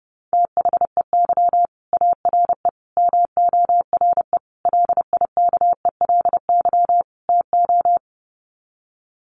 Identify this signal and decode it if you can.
Morse code, heard as a keyed tone, about 30 wpm, 700 Hz. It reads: THEY ARE MORE LIKELY TO
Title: Sets of 5 Words Encore2 30wpm